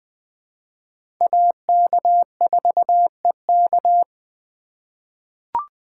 Morse code, heard as a keyed tone, about 21 wpm, 700 Hz. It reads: AK4EK
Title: USA Call Signs - 5 Characters - 2x Character Spacing - 20wpm